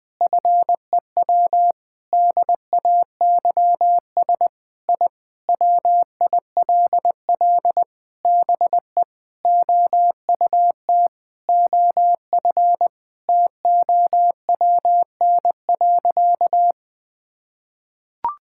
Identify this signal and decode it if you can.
Morse code, heard as a keyed tone, about 20 wpm, 700 Hz. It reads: FEW DAYS I WILL BE OUT OF TOWN.
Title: Sentences with 3-Word Phrases - Spoken Beginning - 20wpm